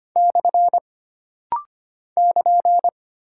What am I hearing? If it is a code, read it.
Morse code, heard as a keyed tone, about 25 wpm, 700 Hz. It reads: / <KN>
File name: Single Punctuation-Prosign - Rapid-Fire - 25wpm